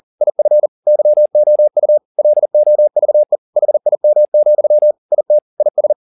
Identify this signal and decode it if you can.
Morse code, heard as a keyed tone, about 40 wpm, 600 Hz. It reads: IF YOU LOVE HIM, IT IS
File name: the old man and the sea 2 40wpm